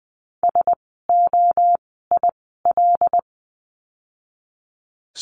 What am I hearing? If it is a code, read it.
Morse code, heard as a keyed tone, about 20 wpm, spaced as for 14 wpm, 700 Hz. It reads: SOIL
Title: 075 Lesson 13f - Letter L - Words 20wpm